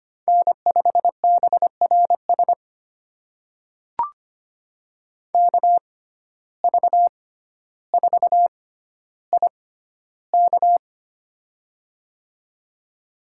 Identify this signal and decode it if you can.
Morse code, heard as a keyed tone, about 25 wpm, spaced as for 13 wpm, 700 Hz. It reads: N5BRS KV4IK
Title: USA Call Signs - 5 Characters - 6x Character Spacing - 25wpm